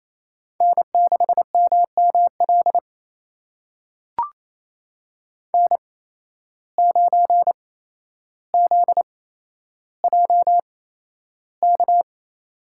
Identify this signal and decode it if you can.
Morse code, heard as a keyed tone, about 28 wpm, spaced as for 12 wpm, 700 Hz. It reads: N6MML N9ZJK
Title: USA Call Signs - 5 Characters - 8x Character Spacing - 28wpm